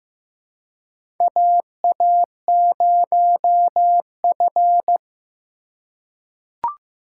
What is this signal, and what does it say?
Morse code, heard as a keyed tone, about 15 wpm, 700 Hz. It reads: AA0F
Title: USA Call Signs - 3x Character Spacing - 15wpm